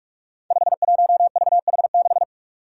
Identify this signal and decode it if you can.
Morse code, heard as a keyed tone, about 44 wpm, 700 Hz. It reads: 51VH6
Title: Five Challenging Letter-Number Combinations 45wpm